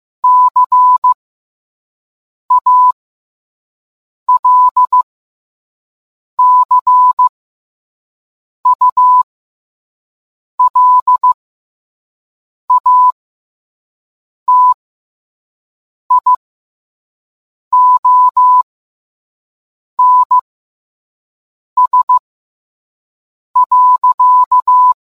Morse code, heard as a keyed tone, about 15 wpm, spaced as for 5 wpm, 1000 Hz. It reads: CALCULATIONS.